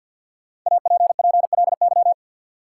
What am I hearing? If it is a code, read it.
Morse code, heard as a keyed tone, about 53 wpm, 700 Hz. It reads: APPLY
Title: Sentences from Top 1200 Words - New Word Drill Alt - 50wpm